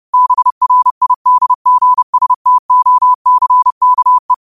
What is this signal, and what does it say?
Morse code, heard as a keyed tone, about 30 wpm, 1000 Hz. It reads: BRIDGSTOCKE
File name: qotdfiles / 30wpm